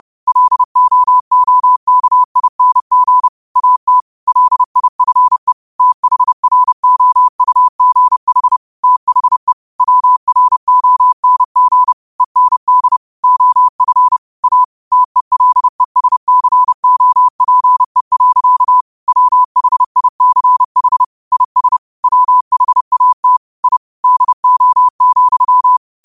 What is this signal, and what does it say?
Morse code, heard as a keyed tone, about 30 wpm, 1000 Hz. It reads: LOOKING AT LIFE THROUGH THE WRONG END OF A TELESCOPE. WHICH IS WHAT I DO,